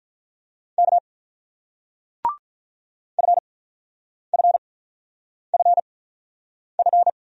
Morse code, heard as a keyed tone, about 46 wpm, spaced as for 9 wpm, 700 Hz. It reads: K FFFF